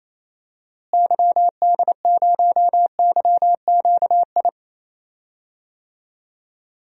Morse code, heard as a keyed tone, about 28 wpm, 700 Hz. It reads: YD0YQI
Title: DXCC-100 Call Signs - 6 Characters - Repeated 2x - 28wpm